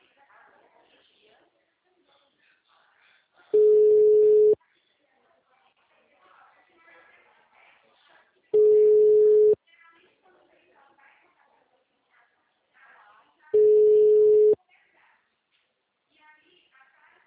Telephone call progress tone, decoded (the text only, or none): ringback tone